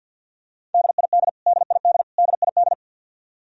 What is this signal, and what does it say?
Morse code, heard as a keyed tone, about 51 wpm, 700 Hz. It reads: DID DID DID
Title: Top 100 Words Repeated 50wpm